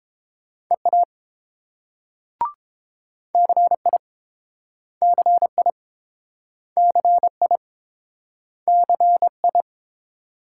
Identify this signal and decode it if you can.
Morse code, heard as a keyed tone, about 30 wpm, 700 Hz. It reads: EA CI CI CI CI